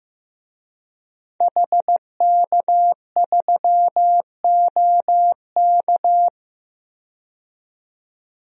Morse code, heard as a keyed tone, about 15 wpm, 700 Hz. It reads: HK3OK
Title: DXCC-100 Call Signs - 5 Characters - 15wpm